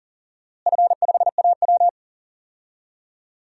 Morse code, heard as a keyed tone, about 37 wpm, 700 Hz. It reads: F5AW
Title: DXCC-100 Call Signs - 4 Characters - Repeated 2x - 40wpm